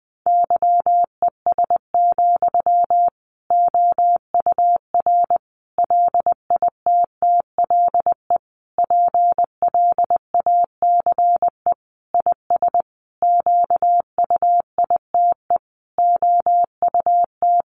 Morse code, heard as a keyed tone, about 20 wpm, 700 Hz. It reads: YES, OUR LITTLE PLACE IS QUITE OUT
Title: Sentences from Top 700 Words 20wpm